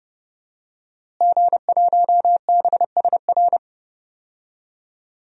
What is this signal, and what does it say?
Morse code, heard as a keyed tone, about 32 wpm, 700 Hz. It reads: G1BSR
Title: DXCC-100 Call Signs - 5 Characters - Repeated 3x - 30wpm